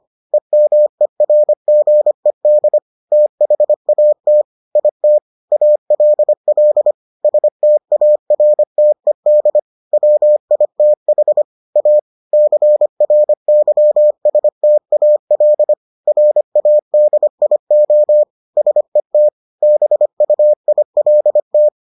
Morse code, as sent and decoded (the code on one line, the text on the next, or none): . -- . .-. --. -.. / - .... .- - / .. - / .- .-.. .-.. / ... - .- .-. - . -.. / .-- .. - .... / .- / -.-. .-. -.-- ... - .- .-.. / .-. .- -.. .. --- / ... - / -... ..- .. .-.. -
EMERGED THAT IT ALL STARTED WITH A CRYSTAL RADIO SET BUILT